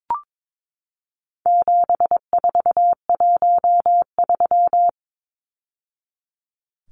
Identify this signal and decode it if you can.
Morse code, heard as a keyed tone, about 22 wpm, 700 Hz. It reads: 7413
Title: Numbers - 4-Digit 22wpm